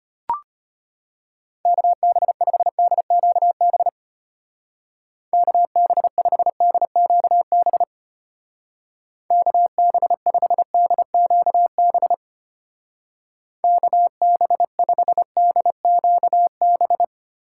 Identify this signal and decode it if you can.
Morse code, heard as a keyed tone, about 32 wpm, 700 Hz. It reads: KB5DQB KB5DQB KB5DQB KB5DQB